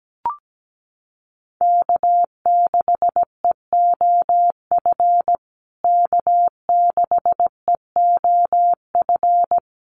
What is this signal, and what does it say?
Morse code, heard as a keyed tone, about 17 wpm, 700 Hz. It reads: K6EOF K6EOF
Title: USA Call Signs Repeated 3x 17wpm